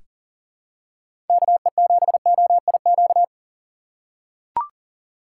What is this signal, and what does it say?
Morse code, heard as a keyed tone, about 40 wpm, 700 Hz. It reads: KE7OIQ
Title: USA Call Signs - 6 Characters - 3x Character Spacing - 40wpm